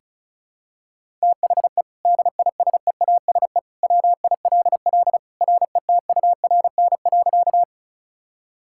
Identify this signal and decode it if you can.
Morse code, heard as a keyed tone, about 36 wpm, 700 Hz. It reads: THE DISEASE WILL RETURN.
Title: Sentences from Top 1000 Words - Review Part 6 - 35wpm